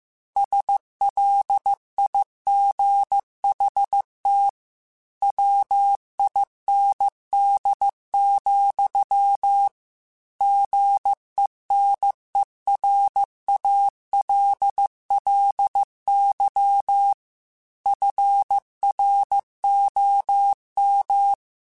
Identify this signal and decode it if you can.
Morse code, heard as a keyed tone, about 15 wpm, 800 Hz. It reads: SLIGHT WIND, GENERALLY FROM
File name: Weather Sets / Weather Sets-mp3s / Weather 4 17 @13